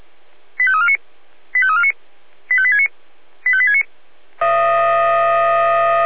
Five-tone selective call (ZVEI), heard as quick runs of five tones, tones of about 70 ms. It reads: 85349 85349 85869 85869